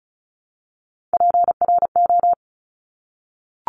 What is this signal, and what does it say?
Morse code, heard as a keyed tone, about 35 wpm, 690 Hz. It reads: PRO
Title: Top Three Letter Combinations in Words 35wpm